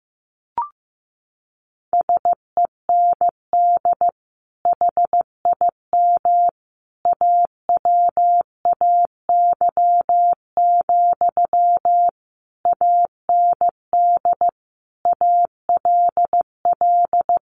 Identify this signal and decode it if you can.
Morse code, heard as a keyed tone, about 14 wpm, 700 Hz. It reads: SEND HIM AWAY, AND ALL
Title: Sentences from Top 1500 Words 15wpm